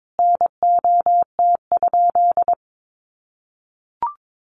Morse code, heard as a keyed tone, about 22 wpm, 700 Hz. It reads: NOT?
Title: Sentences from Top 600 Words 22wpm